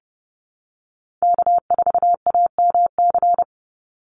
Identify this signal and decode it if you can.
Morse code, heard as a keyed tone, about 30 wpm, 700 Hz. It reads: K4AMC
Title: Call Signs Repeated 2x - 30wpm